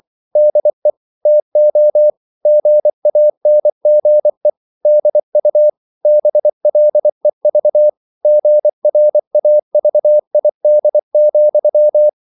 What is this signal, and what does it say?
Morse code, heard as a keyed tone, about 24 wpm, 600 Hz. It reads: DE TO GANGE DU BLEV GRAVID,
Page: Ferie 24 wpm